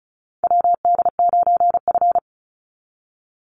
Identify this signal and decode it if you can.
Morse code, heard as a keyed tone, about 34 wpm, 700 Hz. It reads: WD9F
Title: NA SOTA Chaser Callsigns 35wpm